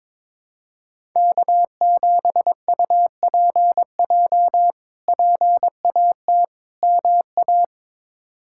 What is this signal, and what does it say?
Morse code, heard as a keyed tone, about 22 wpm, 700 Hz. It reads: K7UPJ PAT MA
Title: K1USN SST QSO Practice - 22wpm